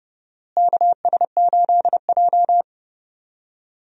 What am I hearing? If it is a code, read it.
Morse code, heard as a keyed tone, about 30 wpm, 700 Hz. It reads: KS8J
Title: USA Call Signs - 4 Characters - Repeated 3x - 30wpm